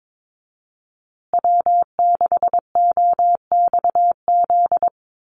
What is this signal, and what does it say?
Morse code, heard as a keyed tone, about 22 wpm, 700 Hz. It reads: W6OXZ
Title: USA Call Signs Repeated 2x 22wpm